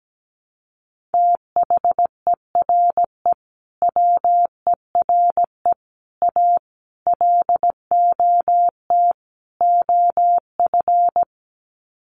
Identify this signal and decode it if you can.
Morse code, heard as a keyed tone, about 17 wpm, 700 Hz. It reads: THERE WERE A LOT OF
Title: Sets of 5 Words 17wpm